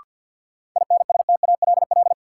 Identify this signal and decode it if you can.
Morse code, heard as a keyed tone, about 53 wpm, 700 Hz. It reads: INSTALL
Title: Sentences from Top 1600 Words - New Word Drill - 50wpm